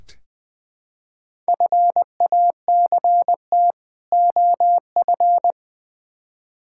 Morse code, heard as a keyed tone, about 20 wpm, 700 Hz. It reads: FACT OF
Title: Build A Phrase - 5-Word Phrases - 20wpm